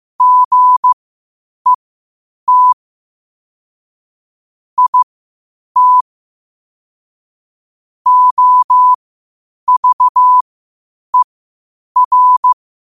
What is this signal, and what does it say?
Morse code, heard as a keyed tone, about 15 wpm, spaced as for 8 wpm, 1000 Hz. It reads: GET IT OVER
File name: qotdfiles / 8wpm